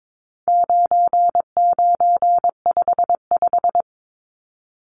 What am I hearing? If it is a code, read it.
Morse code, heard as a keyed tone, about 22 wpm, 700 Hz. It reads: 9955
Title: Numbers - 4-Digit 22wpm